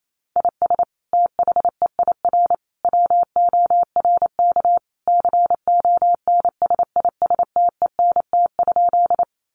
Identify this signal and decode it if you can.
Morse code, heard as a keyed tone, about 28 wpm, 700 Hz. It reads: IS THEIR WORK CONSISTENT?
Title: Sentences from Top 1500 Words 28wpm